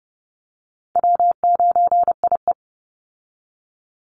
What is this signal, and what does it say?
Morse code, heard as a keyed tone, about 30 wpm, 700 Hz. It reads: W9IE
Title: Easy Call Signs - 30wpm